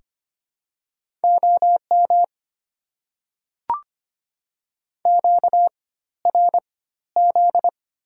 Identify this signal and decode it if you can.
Morse code, heard as a keyed tone, about 25 wpm, spaced as for 14 wpm, 690 Hz. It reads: OM QRZ
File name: Top 90 QSO Elements - 4x Character Spacing - 25wpm